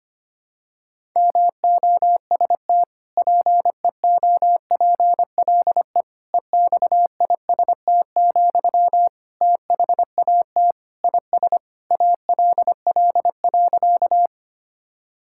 Morse code, heard as a keyed tone, about 25 wpm, 700 Hz. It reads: MOST PEOPLE EXIST, THAT IS ALL.
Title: Sentences from Top 1000 Words - Review Part 5 - 25wpm